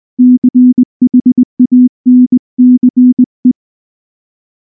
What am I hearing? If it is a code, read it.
Morse code, heard as a keyed tone, about 19 wpm, 250 Hz. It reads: CHANCE
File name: morse / morse-training-20WPM-1